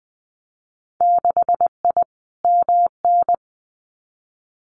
Morse code, heard as a keyed tone, about 20 wpm, 700 Hz. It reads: 6I MN
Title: Winter Field Day Exchanges Encore 20wpm